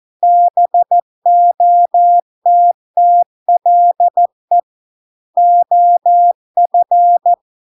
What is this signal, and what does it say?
Morse code, heard as a keyed tone, about 14 wpm, 700 Hz. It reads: BOTTLE OF